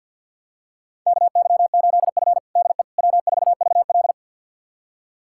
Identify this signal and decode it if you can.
Morse code, heard as a keyed tone, about 52 wpm, 700 Hz. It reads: KY8F DE W4VL
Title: USA Callsign DE Callsign - 4 Characters - 50wpm